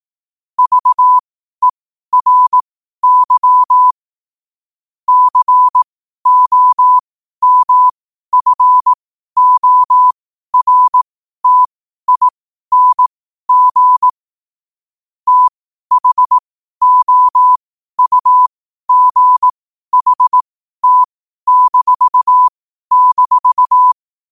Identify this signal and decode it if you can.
Morse code, heard as a keyed tone, about 18 wpm, spaced as for 12 wpm, 1000 Hz. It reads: VERY COMFORTING THOUGHT--
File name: qotdfiles / 12wpm